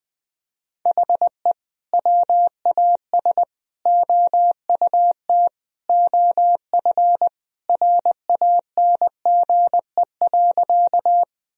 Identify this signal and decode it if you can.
Morse code, heard as a keyed tone, about 20 wpm, 700 Hz. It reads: HE WAS OUT OF RANGE.
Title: Sentences from Top 1000 Words - Review Part 4 - 20wpm